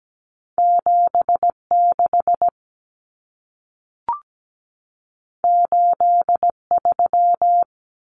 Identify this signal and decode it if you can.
Morse code, heard as a keyed tone, about 17 wpm, 700 Hz. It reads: 76 83
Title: Numbers - 2-Digit 17wpm